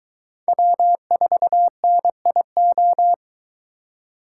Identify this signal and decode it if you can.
Morse code, heard as a keyed tone, about 22 wpm, 700 Hz. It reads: W4NIO